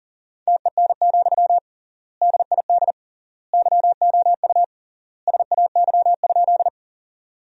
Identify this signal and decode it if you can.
Morse code, heard as a keyed tone, about 40 wpm, 700 Hz. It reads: TEN, DID YOU SAY?
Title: 3x Word-Spacing - Sentences from Top 400 Words - 40wpm